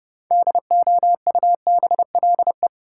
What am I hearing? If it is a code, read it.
Morse code, heard as a keyed tone, about 30 wpm, 700 Hz. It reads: DOUBLE
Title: Sentences from Top 1400 Words - New Word Drill Alt - 30wpm